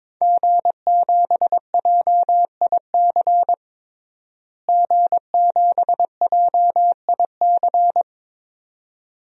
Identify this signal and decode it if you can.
Morse code, heard as a keyed tone, about 22 wpm, 700 Hz. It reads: G7JIC G7JIC
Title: DXCC-100 Call Signs - 5 Characters - Repeated 2x - 22wpm